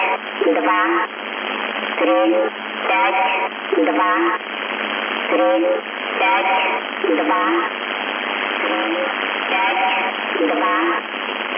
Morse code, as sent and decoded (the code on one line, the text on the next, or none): none